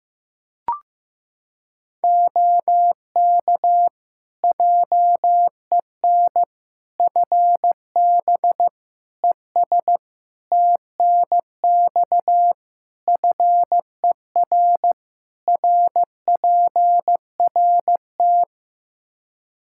Morse code, heard as a keyed tone, about 15 wpm, 700 Hz. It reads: OK JEN FB ES TNX FER RPRT
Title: Practice QSO - Exchange 2 - Info - 15wpm